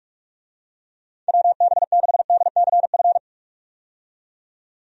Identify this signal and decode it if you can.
Morse code, heard as a keyed tone, about 46 wpm, 700 Hz. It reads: WB6DCF
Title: USA Call Signs - 6 Characters - 45wpm